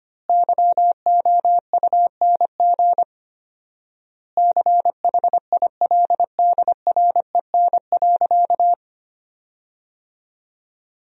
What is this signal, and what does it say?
Morse code, heard as a keyed tone, about 25 wpm, 700 Hz. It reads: YOUNG CHILDREN.
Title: 4x Word-Spacing - Sentences from Top 400 Words - 25wpm